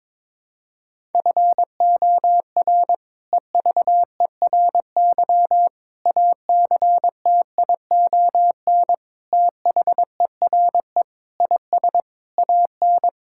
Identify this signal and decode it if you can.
Morse code, heard as a keyed tone, about 22 wpm, 700 Hz. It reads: FOR EVERY ACTION THERE IS AN
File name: practice / 22wpm / Sentences with 8-Word Phrases - 22wpm